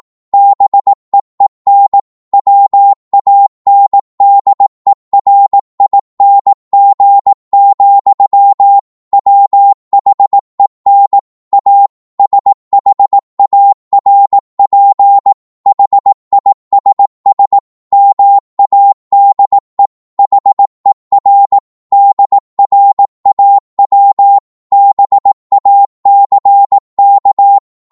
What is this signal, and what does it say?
Morse code, heard as a keyed tone, about 18 wpm, 800 Hz. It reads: BEEN WANDERING, WHEN A SHARP HISS MADE HER DRAW BACK
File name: 18wpm